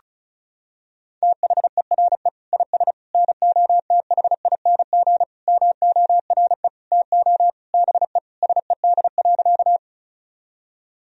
Morse code, heard as a keyed tone, about 36 wpm, 700 Hz. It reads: THERE IS NOTHING MORE TO BE SED.
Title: Sentences from Top 500 Words and Abbreviations 35wpm